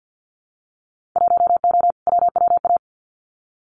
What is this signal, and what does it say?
Morse code, heard as a keyed tone, about 50 wpm, 700 Hz. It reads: 1O WWA
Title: Winter Field Day Exchanges 50wpm